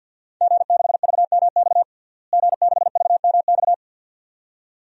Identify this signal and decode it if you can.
Morse code, heard as a keyed tone, about 47 wpm, 700 Hz. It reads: G6VMX G6VMX
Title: DXCC-100 Call Signs - 5 Characters - Repeated 2x - 50wpm